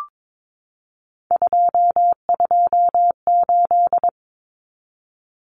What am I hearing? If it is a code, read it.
Morse code, heard as a keyed tone, about 22 wpm, 700 Hz. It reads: 228
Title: Common Cut Numbers - 3-Digit 22wpm